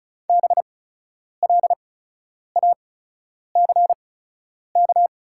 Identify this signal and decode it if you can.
Morse code, heard as a keyed tone, about 36 wpm, spaced as for 10 wpm, 700 Hz. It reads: BLACK
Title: 8x Character Spacing - Top 500 Words - 35wpm